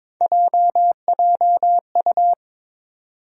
Morse code, heard as a keyed tone, about 22 wpm, 700 Hz. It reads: JJU
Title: USA Call Sign Postfixes - 3 Letters 22wpm